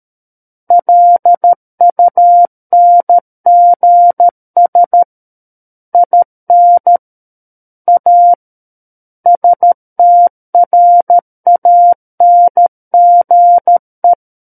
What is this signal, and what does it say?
Morse code, heard as a keyed tone, about 13 wpm, 700 Hz. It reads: LUNGS IN A STRANGE
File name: Book One / 04-Chapter 4 - 13 WPM